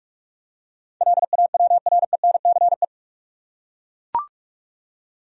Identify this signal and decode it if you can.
Morse code, heard as a keyed tone, about 42 wpm, 700 Hz. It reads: LAWRENCE
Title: USA Common Names - 45wpm